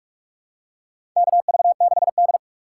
Morse code, heard as a keyed tone, about 44 wpm, 700 Hz. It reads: KV6D
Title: USA Call Signs - 8x Character Spacing - 45wpm